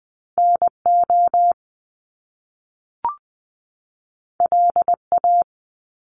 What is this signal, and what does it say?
Morse code, heard as a keyed tone, about 20 wpm, 700 Hz. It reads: NO LA
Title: Top Two Letter Combinations in Words 20wpm